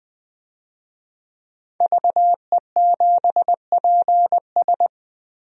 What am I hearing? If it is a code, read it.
Morse code, heard as a keyed tone, about 20 wpm, 700 Hz. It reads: VE7PS
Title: DXCC-100 Call Signs - 5 Characters - 20wpm